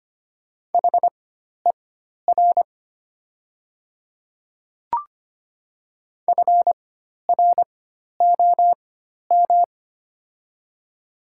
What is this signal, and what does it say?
Morse code, heard as a keyed tone, about 25 wpm, spaced as for 12 wpm, 700 Hz. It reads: HER FROM
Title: 4x Character Spacing - Top 100 Words - 25wpm